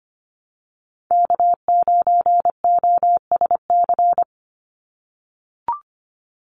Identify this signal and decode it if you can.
Morse code, heard as a keyed tone, about 25 wpm, 700 Hz. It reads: K9OSC
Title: NA SOTA Chaser Callsigns 25wpm